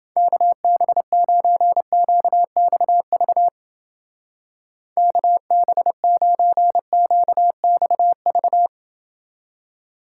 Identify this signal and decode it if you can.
Morse code, heard as a keyed tone, about 29 wpm, 700 Hz. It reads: KB9QXV KB9QXV